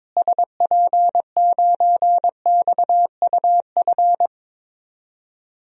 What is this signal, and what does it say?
Morse code, heard as a keyed tone, about 21 wpm, 700 Hz. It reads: SP9XUF
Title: DXCC-100 Call Signs - 6 Characters - Repeated 2x - 22wpm